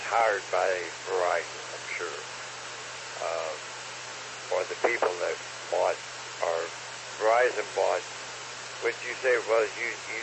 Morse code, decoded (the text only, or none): none